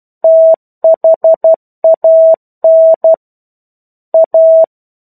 Morse code, heard as a keyed tone, about 12 wpm, 650 Hz. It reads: THAN A
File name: 01-Chapter-01 12 WPM